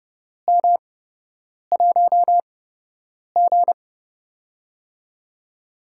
Morse code, heard as a keyed tone, about 30 wpm, spaced as for 8 wpm, 700 Hz. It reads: M1G
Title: USA Call Signs - 4 Characters - 8x Character Spacing - 30wpm